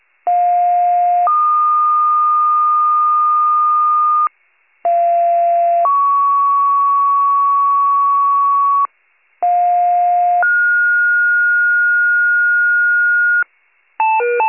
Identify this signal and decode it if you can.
Two-tone page, heard as a one-second tone, then a three-second tone, about 710 then 1220 Hz, about 690 then 1120 Hz, about 710 then 1430 Hz. It is heard from 0.3 to 13.4 s.